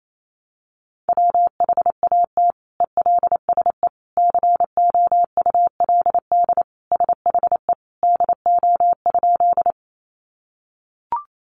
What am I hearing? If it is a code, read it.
Morse code, heard as a keyed tone, about 28 wpm, 700 Hz. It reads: WHAT ELSE COULD SHE DO?
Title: Sentences from Top 800 Words 28wpm